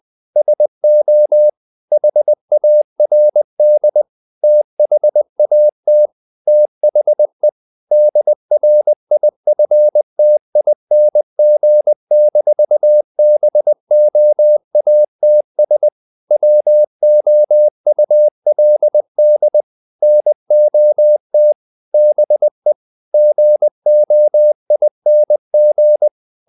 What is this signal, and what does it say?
Morse code, heard as a keyed tone, about 20 wpm, 600 Hz. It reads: SO HARD THAT THE DRIFTING-BOATS WOULD NOT BE GOING
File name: the old man and the sea part 2 20wpm